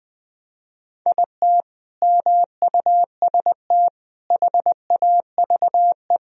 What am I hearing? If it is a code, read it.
Morse code, heard as a keyed tone, about 20 wpm, 700 Hz. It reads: IT MUST HAVE
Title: Sentences from Top 1000 Words - Review Part 1 - 20wpm